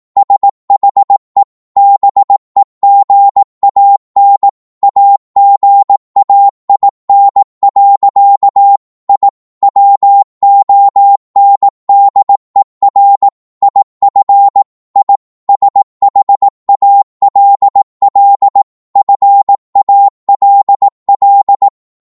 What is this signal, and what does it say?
Morse code, heard as a keyed tone, about 18 wpm, 800 Hz. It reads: SHE BEGAN AGAIN. I WONDER IF I SHALL FALL